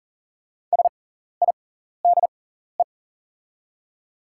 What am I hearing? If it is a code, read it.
Morse code, heard as a keyed tone, about 40 wpm, spaced as for 14 wpm, 700 Hz. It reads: SIDE
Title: 6x Character Spacing - Top 100 Words - 40wpm